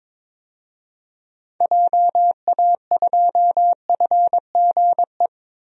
Morse code, heard as a keyed tone, about 22 wpm, 700 Hz. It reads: JA2FGE
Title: DXCC-100 Call Signs - 6 Characters - Repeated 2x - 22wpm